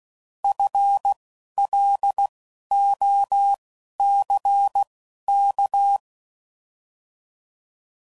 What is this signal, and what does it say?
Morse code, heard as a keyed tone, about 16 wpm, spaced as for 11 wpm, 800 Hz. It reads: FLOCK